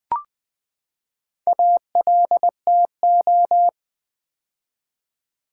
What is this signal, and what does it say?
Morse code, heard as a keyed tone, about 20 wpm, 700 Hz. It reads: ALTO
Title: Build A Word - 4 Letters At A Time - 10-Letter Words - 20wpm